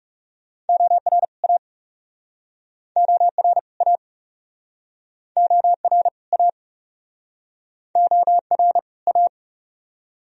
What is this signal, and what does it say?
Morse code, heard as a keyed tone, about 42 wpm, 700 Hz. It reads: OR A OR A OR A OR A